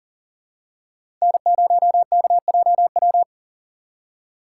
Morse code, heard as a keyed tone, about 40 wpm, 700 Hz. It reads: N0KJW
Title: USA Call Signs - 8x Character Spacing - 40wpm